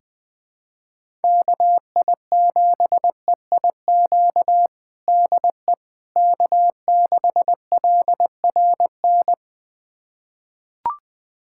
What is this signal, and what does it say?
Morse code, heard as a keyed tone, about 21 wpm, 700 Hz. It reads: KI7EIQ DE K6LRN
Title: USA Callsign DE Callsign 20wpm